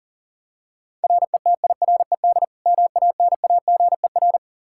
Morse code, heard as a keyed tone, about 37 wpm, 700 Hz. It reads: RETIRED MANAGER
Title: K1USN SST QSO Practice 40wpm